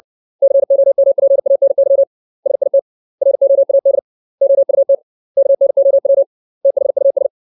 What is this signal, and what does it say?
Morse code, heard as a keyed tone, about 60 wpm, 550 Hz. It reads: COMPANY SIT ROAD OUT KNOW THUS
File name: 632 words / 632 Words 60WPM